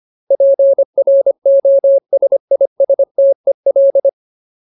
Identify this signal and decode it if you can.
Morse code, heard as a keyed tone, about 25 wpm, 550 Hz. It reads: PROSISTEL
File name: Ultimate Word List / 25WPM Ultimate